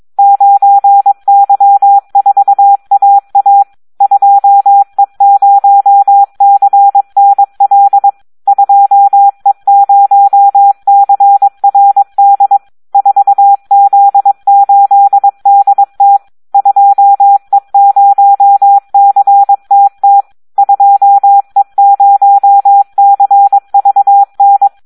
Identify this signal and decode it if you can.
Morse code, heard as a keyed tone, about 22 wpm, 800 Hz. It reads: 9Y4AA 2E0CNL 2E0CRD 4Z8DT 2E0CTT 2E0CVN
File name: Audio practica CW / SD 1 22